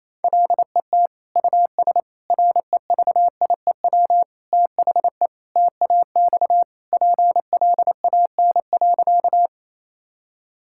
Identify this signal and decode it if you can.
Morse code, heard as a keyed tone, about 27 wpm, 700 Hz. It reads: LET US REVIEW THE TAX PLAN.
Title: Sentences from Top 1000 Words - Review Part 6 - 28wpm